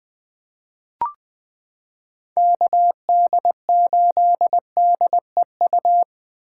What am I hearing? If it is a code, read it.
Morse code, heard as a keyed tone, about 20 wpm, 700 Hz. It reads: KD8DEU
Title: 2023 SOTA Chaser Call Signs - 20wpm